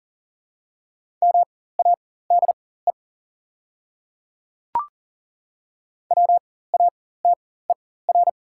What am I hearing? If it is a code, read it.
Morse code, heard as a keyed tone, about 40 wpm, spaced as for 19 wpm, 700 Hz. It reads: MADE WATER
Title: 4x Character Spacing - Top 200 Words - 40wpm